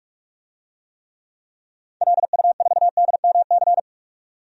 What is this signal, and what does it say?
Morse code, heard as a keyed tone, about 46 wpm, 700 Hz. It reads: LU4DMC
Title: DXCC-100 Call Signs - 6 Characters - 45wpm